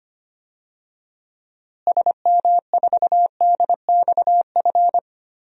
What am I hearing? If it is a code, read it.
Morse code, heard as a keyed tone, about 25 wpm, 700 Hz. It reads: SM4DXF
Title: DXCC-100 Call Signs - 6 Characters - Repeated 2x - 25wpm